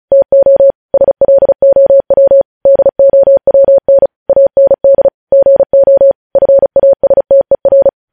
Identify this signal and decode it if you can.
Morse code, heard as a keyed tone, about 36 wpm, 550 Hz. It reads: TO SLOW DOWN AND GO FASTER
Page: News Headlines in Morse Code at 35 WPM